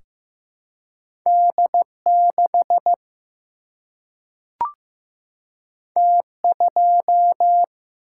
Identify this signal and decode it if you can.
Morse code, heard as a keyed tone, about 15 wpm, 700 Hz. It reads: D6 T2
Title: Single-Letter Single-Number 15wpm